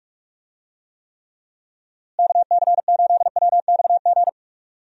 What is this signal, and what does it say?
Morse code, heard as a keyed tone, about 41 wpm, 700 Hz. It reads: KC8WXG
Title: USA Call Signs - 6 Characters - Repeated 2x - 45wpm